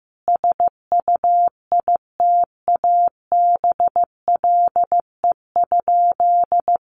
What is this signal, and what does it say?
Morse code, heard as a keyed tone, about 14 wpm, 700 Hz. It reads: SUITABLE?
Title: Sentences from Top 1400 Words 15wpm